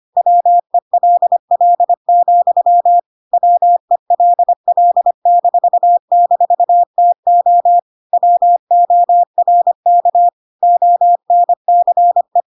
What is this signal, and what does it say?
Morse code, heard as a keyed tone, about 25 wpm, 700 Hz. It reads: WELL, WELL--TO WORK ONCE
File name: Pinocchio / Pinocchio ch1 25 25 ik1hge